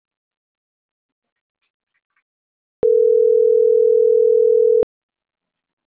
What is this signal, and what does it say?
Ringback tone (North America), 440 + 480 Hz.